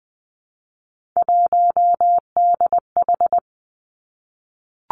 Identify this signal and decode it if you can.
Morse code, heard as a keyed tone, about 20 wpm, 700 Hz. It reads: 1DH
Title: Three Challenging Letter-Number Combinations 20wpm